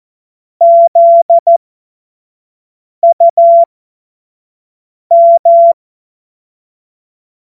Morse code, heard as a keyed tone, about 14 wpm, spaced as for 5 wpm, 680 Hz. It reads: ZUM
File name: Witz-09-September-2025-25-F70